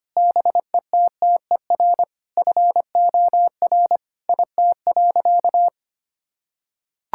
Morse code, heard as a keyed tone, about 25 wpm, 700 Hz. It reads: BETTER FOR IT.
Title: Sentences from Top 1000 Words - Review Part 5 - 25wpm